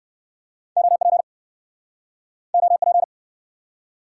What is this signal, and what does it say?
Morse code, heard as a keyed tone, about 69 wpm, 700 Hz. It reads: OP OP